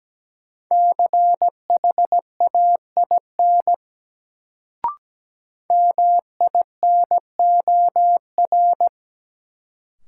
Morse code, heard as a keyed tone, about 17 wpm, 700 Hz. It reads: CHAIN MINOR
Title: Sentences from Top 1500 Words - New Word Drill Alt - 17wpm